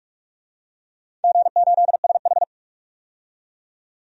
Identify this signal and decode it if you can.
Morse code, heard as a keyed tone, about 44 wpm, 700 Hz. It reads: G8SH
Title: DXCC-100 Call Signs - 4 Characters - 45wpm